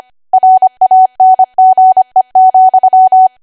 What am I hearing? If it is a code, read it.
Morse code, heard as a keyed tone, about 25 wpm, 750 Hz. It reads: RANGE,
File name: cwa / W1AW-Level3 / 25WPM